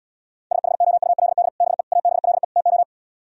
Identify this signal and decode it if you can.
Morse code, heard as a keyed tone, about 77 wpm, 700 Hz. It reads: SHOULD BE ABLE TO